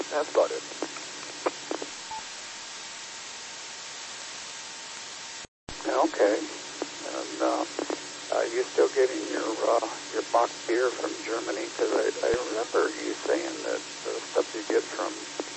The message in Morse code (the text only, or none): none